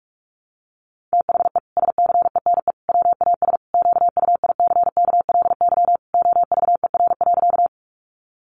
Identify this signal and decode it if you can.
Morse code, heard as a keyed tone, about 44 wpm, 700 Hz. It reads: THE SCENE WAS QUICKLY OVER.
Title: Sentences from Top 1100 Words 45wpm